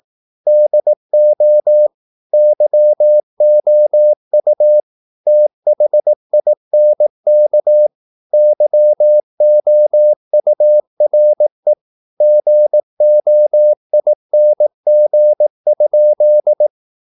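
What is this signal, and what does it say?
Morse code, heard as a keyed tone, about 19 wpm, 600 Hz. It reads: DO YOU THINK YOURE GOING?
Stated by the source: Morse practice at 18 wpm, English text, from CW med Gnister.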